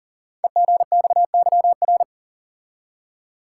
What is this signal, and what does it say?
Morse code, heard as a keyed tone, about 42 wpm, 700 Hz. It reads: EGXYR
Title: Code Groups - 5 Letters - 40wpm